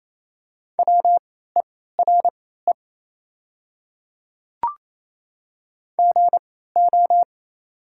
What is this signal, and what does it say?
Morse code, heard as a keyed tone, about 28 wpm, spaced as for 16 wpm, 700 Hz. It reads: WERE GO
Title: 3x Character Spacing - Top 100 Words - 28wpm